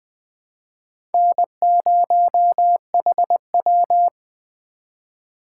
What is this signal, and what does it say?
Morse code, heard as a keyed tone, about 19 wpm, 700 Hz. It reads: N0HW